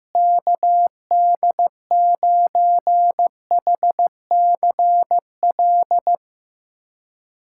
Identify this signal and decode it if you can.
Morse code, heard as a keyed tone, about 15 wpm, 700 Hz. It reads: KD9HCL
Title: USA Callsign DE Callsign - 6 Characters - 15wpm